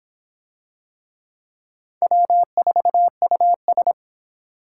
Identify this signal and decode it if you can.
Morse code, heard as a keyed tone, about 23 wpm, 700 Hz. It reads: W4US